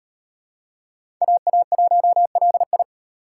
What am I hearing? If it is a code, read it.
Morse code, heard as a keyed tone, about 38 wpm, 700 Hz. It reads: AA1LI